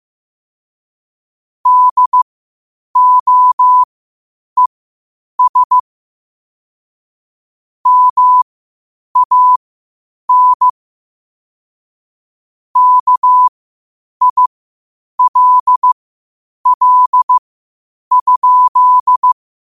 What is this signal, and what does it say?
Morse code, heard as a keyed tone, about 15 wpm, spaced as for 8 wpm, 1000 Hz. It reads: DOES MAN KILL?
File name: qotdfiles / 8wpm